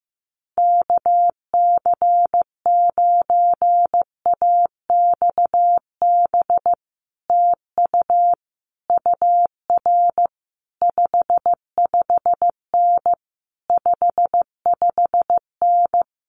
Morse code, heard as a keyed tone, about 15 wpm, 700 Hz. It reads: KC9AXB TU UR 55N 55N
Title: Common QSO Phrases Encore 15wpm